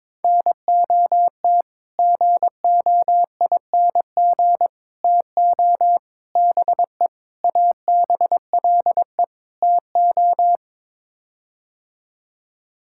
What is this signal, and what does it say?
Morse code, heard as a keyed tone, about 22 wpm, 700 Hz. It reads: NOT GOING TO BE ABLE TO
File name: practice / 22wpm / Sets of 6 Words - 22wpm